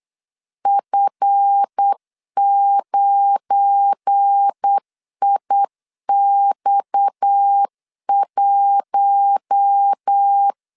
Morse code, heard as a keyed tone, about 8 wpm, 800 Hz. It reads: F9IX1
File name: CW / DET / 1 9 1500C S B 8wpm 5624